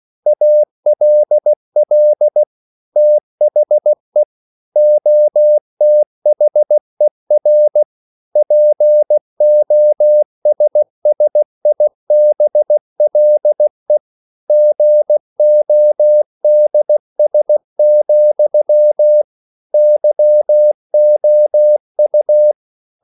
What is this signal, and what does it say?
Morse code, heard as a keyed tone, about 16 wpm, 600 Hz. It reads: ALL THE OTHER POSSIBLE GODS, YOU
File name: CW / Practice / EnglishQuotations0416